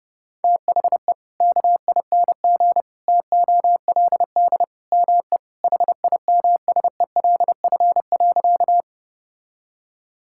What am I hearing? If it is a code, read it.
Morse code, heard as a keyed tone, about 30 wpm, 700 Hz. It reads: THE KING TOLD ME HIMSELF.
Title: Sentences from Top 1600 Words 30wpm